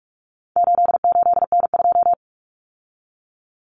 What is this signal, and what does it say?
Morse code, heard as a keyed tone, about 44 wpm, 700 Hz. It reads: 88N2
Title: Common Cut Numbers - 4-Digit 45wpm